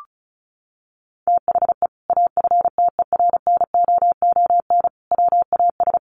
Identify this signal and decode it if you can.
Morse code, heard as a keyed tone, about 34 wpm, 700 Hz. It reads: THE AFTERNOON WAS
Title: Sentences from Top 1500 Words 35wpm